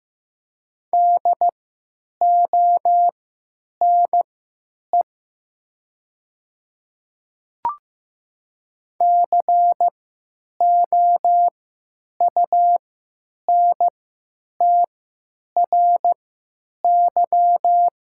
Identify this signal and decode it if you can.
Morse code, heard as a keyed tone, about 15 wpm, spaced as for 9 wpm, 700 Hz. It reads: DONE COUNTRY
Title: 3x Character Spacing - Top 500 Words - 15wpm